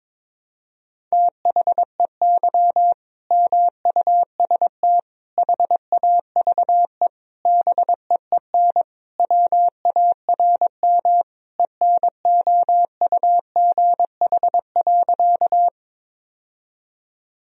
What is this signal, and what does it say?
Morse code, heard as a keyed tone, about 21 wpm, 700 Hz. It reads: THEY MUST HAVE BEEN WARM ENOUGH.
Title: Sentences from Top 1000 Words - Review Part 2 - 22wpm